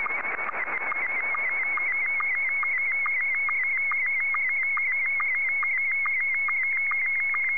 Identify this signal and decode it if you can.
SSTV (slow-scan television), Scottie 1.
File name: sstv / S1-VK2EX